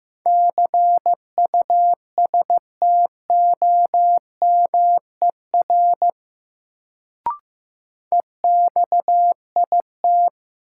Morse code, heard as a keyed tone, about 15 wpm, 700 Hz. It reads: CUSTOMER EXIT
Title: Sentences from Top 1100 Words - New Word Drill Alt - 15wpm